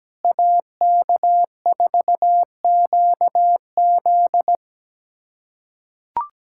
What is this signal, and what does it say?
Morse code, heard as a keyed tone, about 17 wpm, 700 Hz. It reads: AK4QZ
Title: USA Callsign DE Callsign - 5 Characters - 17wpm